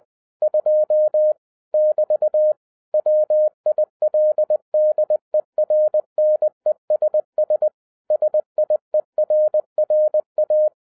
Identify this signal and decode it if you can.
Morse code, heard as a keyed tone, about 20 wpm, 610 Hz. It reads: 2 <BT> WILDERNESS SIERRA
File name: 2011-07-mp3 / n23 QRP OTAKU 20wpm